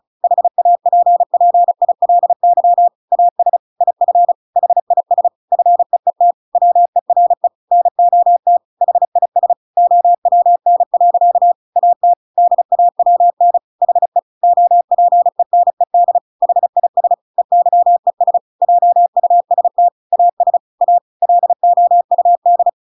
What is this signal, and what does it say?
Morse code, heard as a keyed tone, about 34 wpm, 700 Hz. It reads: HAPPILY AS IF HIS FEET WERE NOT HIS OWN. AT DAWN HE OPENED HIS EYES JUST AS A LOUD